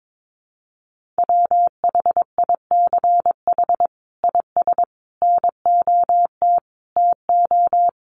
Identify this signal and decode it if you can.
Morse code, heard as a keyed tone, about 22 wpm, 700 Hz. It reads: WHICH IS NOT TO
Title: Sentences from Top 1400 Words 22wpm